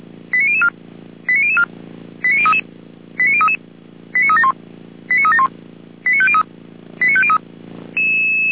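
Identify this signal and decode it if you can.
Five-tone selective call (ZVEI), heard as quick runs of five tones, tones of about 70 ms. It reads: 89004 89004 89033 89033 89371 89371 89503 89503
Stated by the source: ZVEI five-tone call 89010 received on an SDR, from audio.f85360.feuerwehr-orlamuende.org (Feuerwehr Orlamünde fire service).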